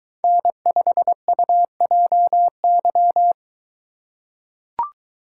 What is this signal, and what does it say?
Morse code, heard as a keyed tone, about 22 wpm, 700 Hz. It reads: N5UJY